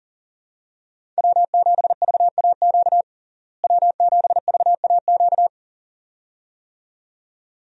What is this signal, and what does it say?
Morse code, heard as a keyed tone, about 40 wpm, 700 Hz. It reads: W7VAQ W7VAQ
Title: USA Call Signs - 5 Characters - Repeated 2x - 40wpm